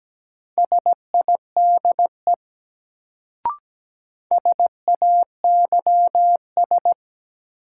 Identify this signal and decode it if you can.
Morse code, heard as a keyed tone, about 17 wpm, 700 Hz. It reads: SIDE SAYS
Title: Sets of 2 Words - New Word Drill - 17wpm